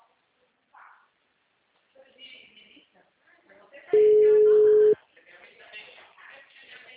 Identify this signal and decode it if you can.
Ringback tone (Europe, 425 Hz), with speech.
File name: out- -2003- - - .12025